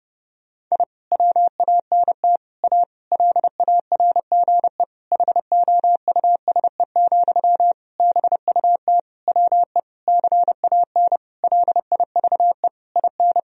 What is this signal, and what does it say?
Morse code, heard as a keyed tone, about 30 wpm, 700 Hz. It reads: I WANT A LARGE HOUSE, BUT WE CAN LIVE IN
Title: Sentences from Top 1000 Words - Review Part 1 - 30wpm